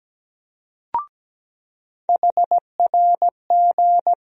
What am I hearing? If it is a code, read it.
Morse code, heard as a keyed tone, about 17 wpm, 700 Hz. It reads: HRG
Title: USA Call Sign Postfixes - 3 Letters 17wpm